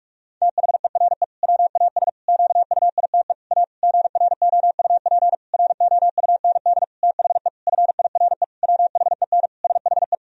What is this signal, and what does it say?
Morse code, heard as a keyed tone, about 46 wpm, 700 Hz. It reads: THERE WAS QUITE A GROUP ROUND THE FIRE WHEN SHE
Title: Sentences from Top 1000 Words - Review Part 1 - 45wpm